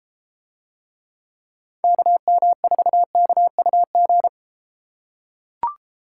Morse code, heard as a keyed tone, about 32 wpm, 700 Hz. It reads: KM4KUG